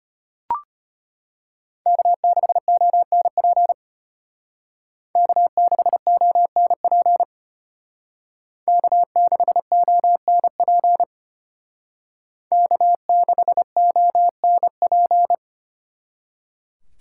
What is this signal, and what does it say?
Morse code, heard as a keyed tone, about 30 wpm, 700 Hz. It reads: K6ONP K6ONP K6ONP K6ONP
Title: Speed Racing - USA 1x3 Call Signs 25wpm